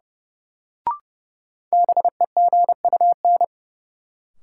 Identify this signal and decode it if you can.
Morse code, heard as a keyed tone, about 30 wpm, 700 Hz. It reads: BEGUN
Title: Sentences from Top 1000 Words - Review Part 6 - New Word Drill - 30wpm